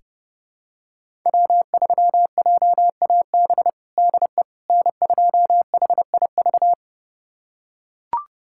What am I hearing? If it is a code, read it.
Morse code, heard as a keyed tone, about 31 wpm, 700 Hz. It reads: W3JAB DE N2HIV
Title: USA Callsign DE Callsign - 5 Characters - 30wpm